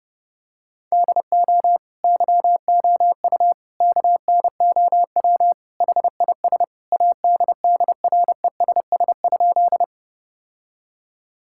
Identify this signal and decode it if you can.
Morse code, heard as a keyed tone, about 30 wpm, 700 Hz. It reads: DO YOU KNOW HIS ADDRESS?
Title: Sentences from Top 1000 Words - Review Part 6 - 30wpm